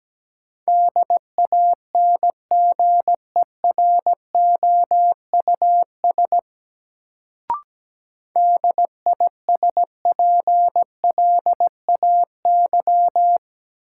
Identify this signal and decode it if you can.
Morse code, heard as a keyed tone, about 17 wpm, 700 Hz. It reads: DANGEROUS DISPLAY
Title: Sentences from Top 1200 Words - New Word Drill - 17wpm